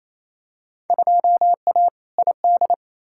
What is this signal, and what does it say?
Morse code, heard as a keyed tone, about 28 wpm, 700 Hz. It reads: 2A ID
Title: Field Day Exchanges Encore - 28wpm